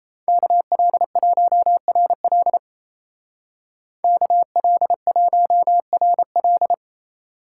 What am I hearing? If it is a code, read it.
Morse code, heard as a keyed tone, about 34 wpm, 700 Hz. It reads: KL1RL KL1RL
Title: Speed Racing - USA Call Signs 28wpm